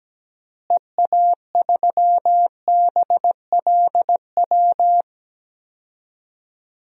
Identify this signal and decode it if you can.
Morse code, heard as a keyed tone, about 17 wpm, 700 Hz. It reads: EA3BLW
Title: DXCC-100 Call Signs - 6 Characters - Repeated 3x - 17wpm